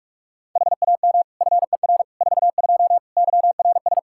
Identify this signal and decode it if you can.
Morse code, heard as a keyed tone, about 45 wpm, 700 Hz. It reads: HAM FER 42 YRS
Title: Practice QSO - Exchange 3 - More Info - 45wpm